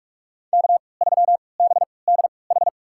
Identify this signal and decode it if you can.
Morse code, heard as a keyed tone, about 46 wpm, spaced as for 25 wpm, 700 Hz. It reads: K3BDH
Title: USA Call Signs - 5 Characters - 3x Character Spacing - 45wpm